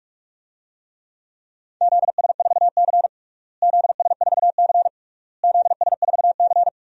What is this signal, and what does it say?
Morse code, heard as a keyed tone, about 41 wpm, 700 Hz. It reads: ZS4C ZS4C ZS4C
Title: DXCC-100 Call Signs - 4 Characters - Repeated 3x - 45wpm